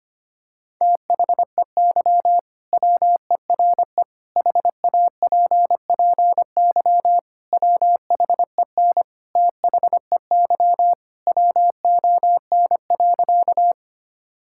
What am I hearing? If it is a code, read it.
Morse code, heard as a keyed tone, about 25 wpm, 690 Hz. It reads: THEY WERE HAPPY WHEN THEY WON.
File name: practice / 25wpm / Sentences with 2-Word Phrases - 25wpm